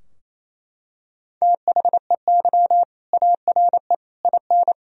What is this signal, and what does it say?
Morse code, heard as a keyed tone, about 27 wpm, 690 Hz. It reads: THEY ARE IN
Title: Sentences from Top 1000 Words - Review Part 3 - 28wpm